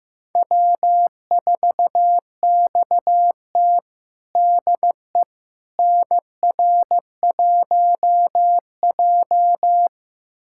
Morse code, heard as a keyed tone, about 15 wpm, 700 Hz. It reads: W4XT DE NR1J
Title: USA Callsign DE Callsign - 4 Characters - 15wpm